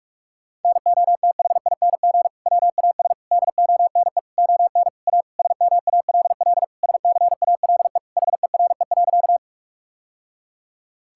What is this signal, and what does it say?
Morse code, heard as a keyed tone, about 46 wpm, 700 Hz. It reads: NOTHING WAS DONE ON A SMALL SCALE HERE.
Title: Sentences from Top 1000 Words - Review Part 5 - 45wpm